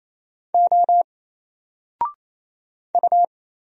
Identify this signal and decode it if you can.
Morse code, heard as a keyed tone, about 27 wpm, 700 Hz. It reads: O U
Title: Single Letter-Number Alt 28wpm